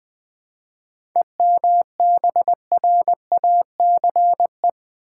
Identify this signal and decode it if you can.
Morse code, heard as a keyed tone, about 20 wpm, 700 Hz. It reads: EMBRACE
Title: 7 Letter Words - First 3 Letters Given - 20wpm